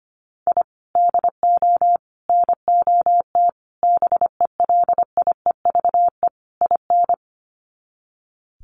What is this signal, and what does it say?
Morse code, heard as a keyed tone, about 25 wpm, 700 Hz. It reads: I DO NOT BELIEVE IN
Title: Sets of 5 Words Encore 25wpm